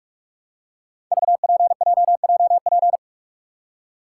Morse code, heard as a keyed tone, about 44 wpm, 700 Hz. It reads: VPJJP
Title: Code Groups - 5 Letters - 45wpm